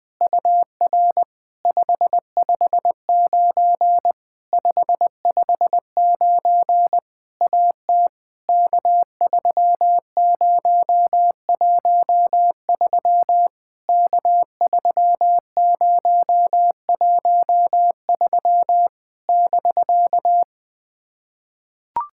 Morse code, heard as a keyed tone, about 20 wpm, 700 Hz. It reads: UR 559 559 AT K3013 K3013 <BK>
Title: POTA QSO Practice - 20wpm